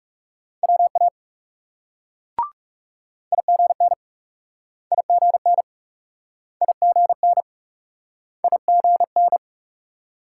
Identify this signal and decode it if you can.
Morse code, heard as a keyed tone, about 38 wpm, 700 Hz. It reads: WA IGN IGN IGN IGN